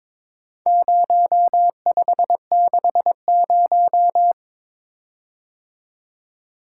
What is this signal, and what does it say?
Morse code, heard as a keyed tone, about 22 wpm, 700 Hz. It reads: 0560
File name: practice / 22wpm / Four Letter-Number Combinations - 22wpm